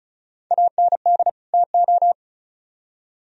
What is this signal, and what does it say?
Morse code, heard as a keyed tone, about 48 wpm, spaced as for 42 wpm, 700 Hz. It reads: AND TO
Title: Speed-Racing - Sets of 2 Words 35wpm